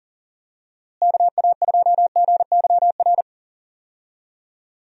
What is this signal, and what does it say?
Morse code, heard as a keyed tone, about 39 wpm, 700 Hz. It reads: KA2GYR
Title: USA Call Signs - 6 Characters - Repeated 3x - 40wpm